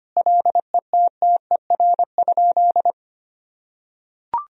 Morse code, heard as a keyed tone, about 25 wpm, 700 Hz. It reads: LETTER?
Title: Sentences from Top 1000 Words - Review Part 3 - 25wpm